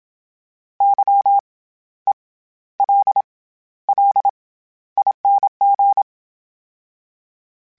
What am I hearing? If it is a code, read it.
Morse code, heard as a keyed tone, about 27 wpm, 800 Hz. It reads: Y E L L ING
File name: Special INGs-mp3s / Special INGs-mp3s / SlowToFaster ing @25